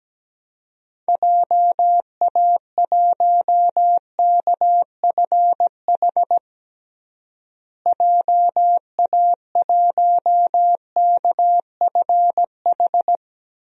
Morse code, heard as a keyed tone, about 17 wpm, 700 Hz. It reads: JA1KFH JA1KFH
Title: DXCC-100 Call Signs - 6 Characters - Repeated 3x - 17wpm